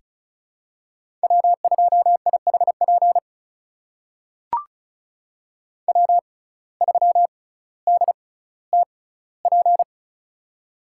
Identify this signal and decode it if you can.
Morse code, heard as a keyed tone, about 34 wpm, spaced as for 18 wpm, 700 Hz. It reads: W2IHP W3DTP
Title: USA Call Signs - 5 Characters - 6x Character Spacing - 35wpm